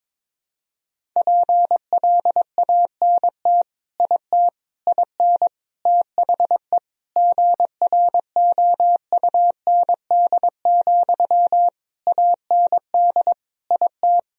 Morse code, heard as a keyed tone, about 21 wpm, 700 Hz. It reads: PLANT IT IN THE GROUND, AND IT
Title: Sentences from Top 1000 Words - Review Part 2 - 22wpm